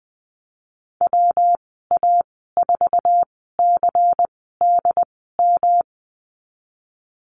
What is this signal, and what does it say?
Morse code, heard as a keyed tone, about 20 wpm, spaced as for 15 wpm, 700 Hz. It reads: WA4CDM
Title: 128 Lesson 21g - Number 6 - Call Signs 20wpm